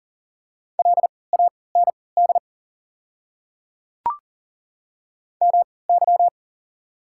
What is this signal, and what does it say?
Morse code, heard as a keyed tone, about 40 wpm, spaced as for 23 wpm, 690 Hz. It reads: LAND MY
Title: 3x Character Spacing - Top 300 Words - 40wpm